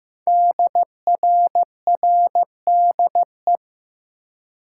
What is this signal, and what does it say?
Morse code, heard as a keyed tone, about 15 wpm, 700 Hz. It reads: DRRDE
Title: Code Groups - 5 Letters - 15wpm